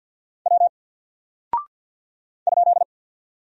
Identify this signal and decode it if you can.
Morse code, heard as a keyed tone, about 50 wpm, 700 Hz. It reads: W ?
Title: Single Character (Letter:Number:Punctuation:Prosign) 50wpm